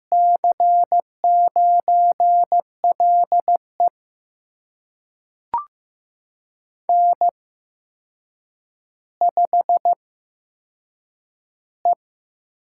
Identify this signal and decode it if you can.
Morse code, heard as a keyed tone, about 15 wpm, spaced as for 7 wpm, 700 Hz. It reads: C9LE N5E
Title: USA Call Signs - 5 Characters - 8x Character Spacing - 15wpm